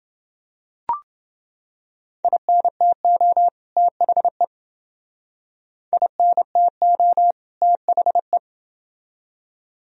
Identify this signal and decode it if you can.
Morse code, heard as a keyed tone, about 29 wpm, 700 Hz. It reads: INTO THE INTO THE